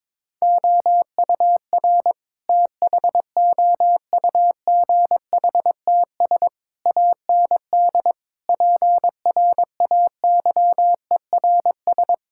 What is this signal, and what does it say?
Morse code, heard as a keyed tone, about 22 wpm, 700 Hz. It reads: OUR THOUGHTS AND PRAYERS
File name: practice / 22wpm / Sets of 8 Words - 22wpm